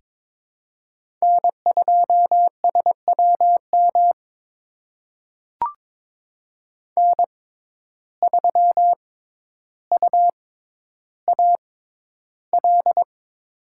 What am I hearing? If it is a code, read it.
Morse code, heard as a keyed tone, about 23 wpm, spaced as for 11 wpm, 700 Hz. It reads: N2SWM N3UAL
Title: USA Call Signs - 5 Characters - 6x Character Spacing - 22wpm